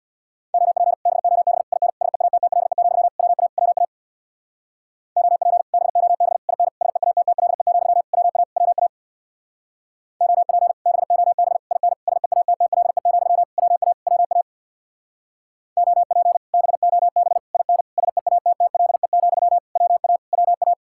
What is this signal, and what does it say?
Morse code, heard as a keyed tone, about 66 wpm, 700 Hz. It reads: OP BOB IN SEATTLE, WA WA OP BOB IN SEATTLE, WA WA OP BOB IN SEATTLE, WA WA OP BOB IN SEATTLE, WA WA